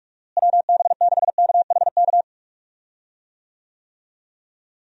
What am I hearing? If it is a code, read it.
Morse code, heard as a keyed tone, about 42 wpm, 700 Hz. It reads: WB6KHK
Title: USA Call Signs - 6 Characters - Repeated 3x - 45wpm